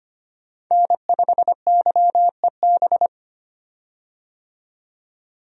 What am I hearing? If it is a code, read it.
Morse code, heard as a keyed tone, about 25 wpm, 700 Hz. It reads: N5YEB
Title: USA Call Signs - 5 Characters - 25wpm